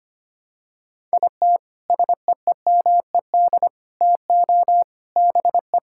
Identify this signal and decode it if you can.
Morse code, heard as a keyed tone, about 25 wpm, 700 Hz. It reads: IT SEEMED TO BE
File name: practice / 25wpm / Sentences with 3-Word Phrases - 25wpm